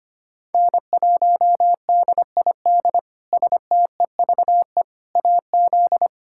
Morse code, heard as a keyed tone, about 25 wpm, 700 Hz. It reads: N1DID STEVE AZ
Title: K1USN SST QSO Practice - 25wpm